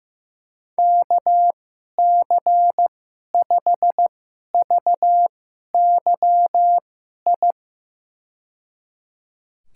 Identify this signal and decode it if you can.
Morse code, heard as a keyed tone, about 15 wpm, spaced as for 11 wpm, 700 Hz. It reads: KC5VYI
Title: USA Call Signs - 6 Characters - 2x Character Spacing - 15wpm